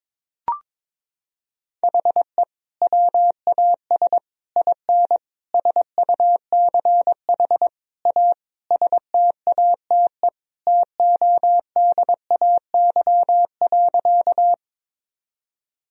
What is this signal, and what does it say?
Morse code, heard as a keyed tone, about 22 wpm, 700 Hz. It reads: HE WAS IN SUCH A STATE TODAY.
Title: Sentences from Top 1000 Words - Review Part 2 - 22wpm